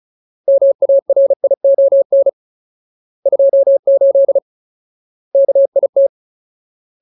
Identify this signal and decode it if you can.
Morse code, heard as a keyed tone, about 36 wpm, 550 Hz. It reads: MARION 28 KIT
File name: Ultimate Word List / 35WPM Ultimate